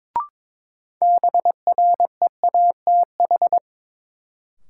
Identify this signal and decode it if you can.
Morse code, heard as a keyed tone, about 22 wpm, 700 Hz. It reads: BREATH
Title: Sets of 5 Words Encore - New Word Drill - 22wpm